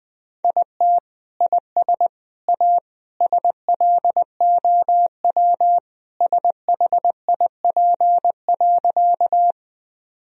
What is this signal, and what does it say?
Morse code, heard as a keyed tone, about 20 wpm, 700 Hz. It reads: IT IS A SLOW SHIP.
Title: Sentences from Top 1000 Words - Review Part 2 - 20wpm